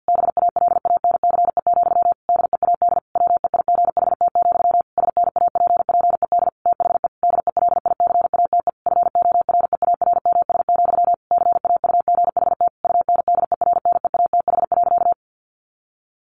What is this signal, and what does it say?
Morse code, heard as a keyed tone, about 48 wpm, 700 Hz. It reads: BALANCE, DEAD WEIGHT, SNAPPED THE DELICATE FOREARMS, CAUGHT UNDERNEATH.
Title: A Sound Of Thunder 50wpm